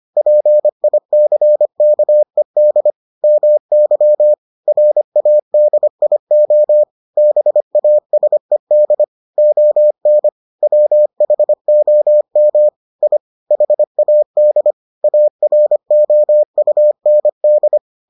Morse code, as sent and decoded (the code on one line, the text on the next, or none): .--. .. -.-. -.- . -.. / -- -.-- / .-. .- -.. .. --- / -... .- ... -.. / --- -. / .-- .... --- -- / .. / .... .- -.. / .- .-. --- ..- -. -..
PICKED MY RADIO BASED ON WHOM I HAD AROUND